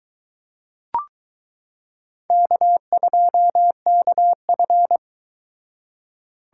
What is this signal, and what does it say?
Morse code, heard as a keyed tone, about 21 wpm, 700 Hz. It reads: K2KF